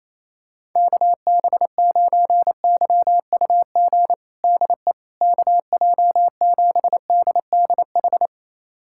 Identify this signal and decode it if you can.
Morse code, heard as a keyed tone, about 28 wpm, 700 Hz. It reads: KB9YUG DE KJ7DDH
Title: USA Callsign DE Callsign - 6 Characters - 28wpm